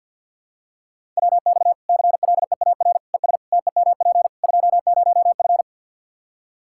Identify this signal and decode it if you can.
Morse code, heard as a keyed tone, about 46 wpm, 700 Hz. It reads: WX CLEAR ES TEMP 20F
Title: Practice QSO - Exchange 2 - Info - 50wpm